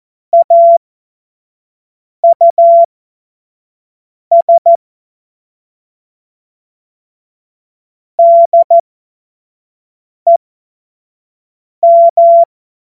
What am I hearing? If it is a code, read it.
Morse code, heard as a keyed tone, about 14 wpm, spaced as for 5 wpm, 680 Hz. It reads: AUS DEM
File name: Morsekurs / Witze / Witz 08. November 2022 25 F70